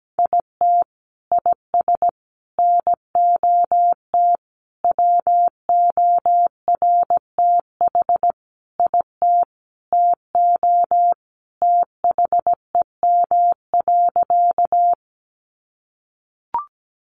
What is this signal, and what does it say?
Morse code, heard as a keyed tone, about 17 wpm, 700 Hz. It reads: IT IS NOT WORTH IT TO THEM.
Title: Sentences from Top 800 Words 17wpm